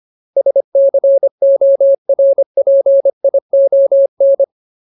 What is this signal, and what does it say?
Morse code, heard as a keyed tone, about 25 wpm, 550 Hz. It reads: SCORPION
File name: Ultimate Word List / 25WPM Ultimate